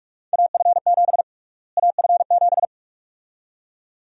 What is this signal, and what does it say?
Morse code, heard as a keyed tone, about 46 wpm, 700 Hz. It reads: AF7 AF7
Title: USA Call Sign Prefixes - Repeated 2x - 45wpm